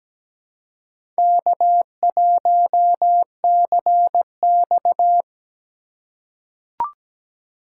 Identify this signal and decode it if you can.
Morse code, heard as a keyed tone, about 17 wpm, 700 Hz. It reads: K1CX
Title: K1USN SST QSO Practice - 17wpm